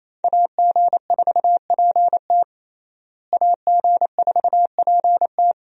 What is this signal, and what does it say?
Morse code, heard as a keyed tone, about 28 wpm, 700 Hz. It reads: AG4PT AG4PT
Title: USA Call Signs - 5 Characters - Repeated 2x - 28wpm